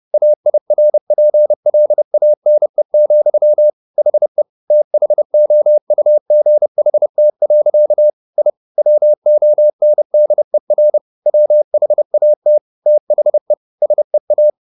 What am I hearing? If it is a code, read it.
Morse code, heard as a keyed tone, about 30 wpm, 600 Hz. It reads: AIRPLANE, HE THOUGHT. I WONDER WHAT THE SEA